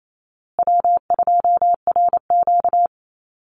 Morse code, heard as a keyed tone, about 28 wpm, 700 Hz. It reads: W2RQ
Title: Call Signs Repeated 2x - 28wpm